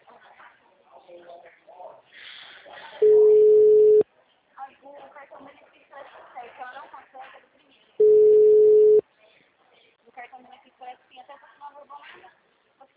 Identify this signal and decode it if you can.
Ringback tone (Europe, 425 Hz), with speech.
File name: out- -2019